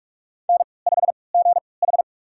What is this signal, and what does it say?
Morse code, heard as a keyed tone, about 47 wpm, spaced as for 26 wpm, 700 Hz. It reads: N5GH
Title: USA Call Signs - 4 Characters - 3x Character Spacing - 45wpm